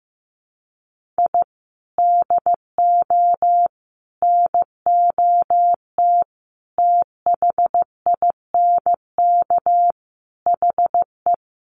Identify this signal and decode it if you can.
Morse code, heard as a keyed tone, about 15 wpm, 700 Hz. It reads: I DO NOT THINK HE
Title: Sets of 5 Words 15wpm